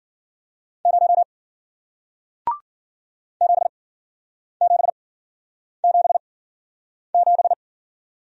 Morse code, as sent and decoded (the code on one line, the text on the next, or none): ----- / --... --... --... --...
0 7777